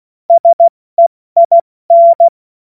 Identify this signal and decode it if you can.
Morse code, heard as a keyed tone, about 16 wpm, 680 Hz. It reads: SEIN
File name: Morsekurs / Witze / Witz 06. September 2022 70 F80